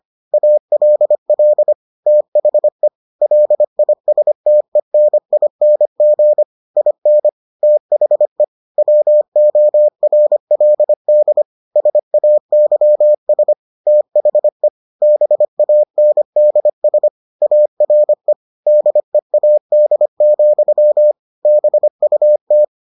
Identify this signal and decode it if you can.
Morse code, heard as a keyed tone, about 25 wpm, 600 Hz. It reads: ALL THE LISTENING IN THE WORLD SAYS THE BANDS ARE DEAD, BUT